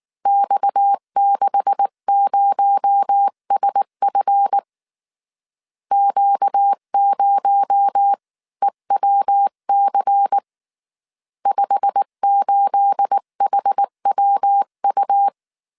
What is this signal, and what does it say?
Morse code, heard as a keyed tone, about 19 wpm, 800 Hz. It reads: X60SF Q0EWC 58HWU
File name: CW / DET / 1 9 1500C S B 18wpm 2551